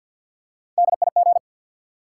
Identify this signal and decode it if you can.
Morse code, heard as a keyed tone, about 52 wpm, 700 Hz. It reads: DIG
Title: Three Letter Words - 50wpm